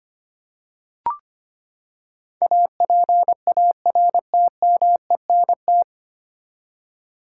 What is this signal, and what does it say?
Morse code, heard as a keyed tone, about 25 wpm, 700 Hz. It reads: APARTMENT
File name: Nine Letter Words - 25wpm